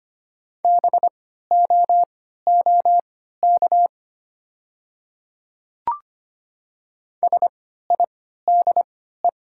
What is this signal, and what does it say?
Morse code, heard as a keyed tone, about 25 wpm, spaced as for 14 wpm, 700 Hz. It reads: BOOK SIDE
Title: 3x Character Spacing - Top 300 Words - 25wpm